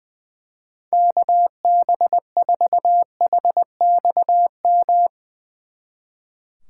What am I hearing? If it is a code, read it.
Morse code, heard as a keyed tone, about 22 wpm, 700 Hz. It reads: KB4HXM
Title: Speed Racing - USA 2x3 Call Signs 20wpm